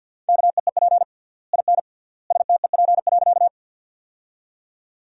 Morse code, heard as a keyed tone, about 51 wpm, 700 Hz. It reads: KEEP IN STEP.
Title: 3x Word-Spacing - Sentences from Top 400 Words - 50wpm